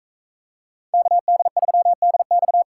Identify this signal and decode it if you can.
Morse code, heard as a keyed tone, about 42 wpm, 700 Hz. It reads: KD3DX